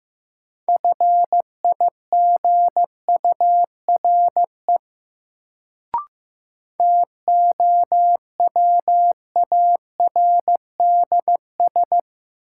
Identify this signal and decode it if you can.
Morse code, heard as a keyed tone, about 16 wpm, 700 Hz. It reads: FIGURE TOWARDS
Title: Sentences from Top 1000 Words - Review Part 2 - New Word Drill - 15wpm